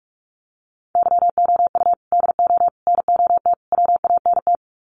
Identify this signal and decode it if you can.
Morse code, heard as a keyed tone, about 44 wpm, 700 Hz. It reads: YOU DO NOT WANT
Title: Sets of 4 Words 45wpm